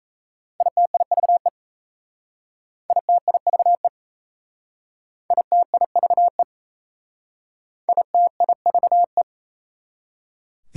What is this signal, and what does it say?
Morse code, heard as a keyed tone, about 36 wpm, 690 Hz. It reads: ITIVE ITIVE ITIVE ITIVE